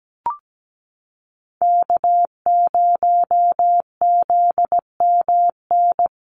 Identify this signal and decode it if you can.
Morse code, heard as a keyed tone, about 17 wpm, 700 Hz. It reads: K0ZMN
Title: USA Call Signs Repeated 2x 17wpm